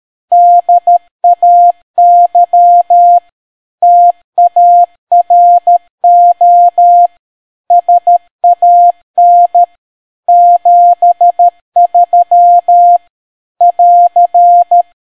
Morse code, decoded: DAY TARO SAN 73 <AR>